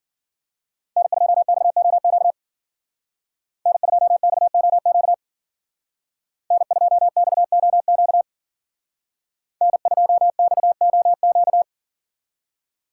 Morse code, heard as a keyed tone, about 51 wpm, 700 Hz. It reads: N2XOQ N2XOQ N2XOQ N2XOQ